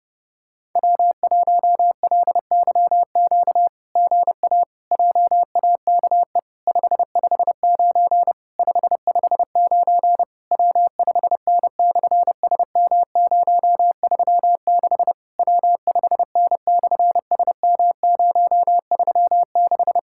Morse code, heard as a keyed tone, about 30 wpm, 700 Hz. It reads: W1LYQ GA JAKE 559 559 W5N/SM036 W5N/SM036
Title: SOTA QSO Practice 30wpm